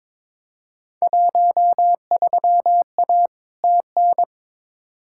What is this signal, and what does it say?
Morse code, heard as a keyed tone, about 22 wpm, 700 Hz. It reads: 13A TN
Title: Field Day Exchanges Encore - 22wpm